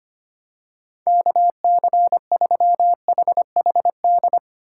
Morse code, heard as a keyed tone, about 25 wpm, 700 Hz. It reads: KC3HHD
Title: USA Call Signs - 6 Characters - 2x Character Spacing - 25wpm